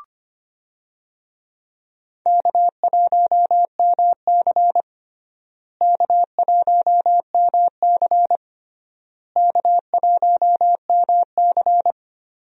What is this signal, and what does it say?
Morse code, heard as a keyed tone, about 25 wpm, 700 Hz. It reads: K1MC K1MC K1MC
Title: USA Call Signs - 4 Characters - Repeated 3x - 25wpm